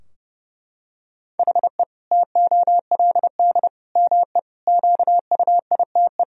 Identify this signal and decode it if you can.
Morse code, heard as a keyed tone, about 30 wpm, 700 Hz. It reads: HE TOLD ME QUITE
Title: Sentences from Top 1000 Words - Review Part 3 - 30wpm